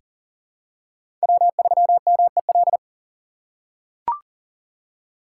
Morse code, heard as a keyed tone, about 39 wpm, 700 Hz. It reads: W3MEL
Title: K1USN SST QSO Practice - 40wpm